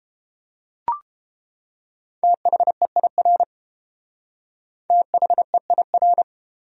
Morse code, heard as a keyed tone, about 29 wpm, 700 Hz. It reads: THEIR THEIR